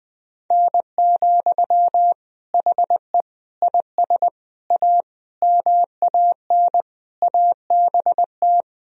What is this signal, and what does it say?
Morse code, heard as a keyed tone, about 20 wpm, 700 Hz. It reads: N, HE IS A MAN ABT
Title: Sentences from Top 400 Words and Abbreviations 20wpm